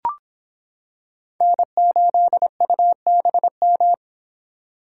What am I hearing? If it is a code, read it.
Morse code, heard as a keyed tone, about 26 wpm, 700 Hz. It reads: N8UBM